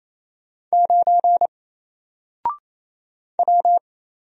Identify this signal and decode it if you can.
Morse code, heard as a keyed tone, about 28 wpm, 700 Hz. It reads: 9 W
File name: Single-Letter-Number-28wpm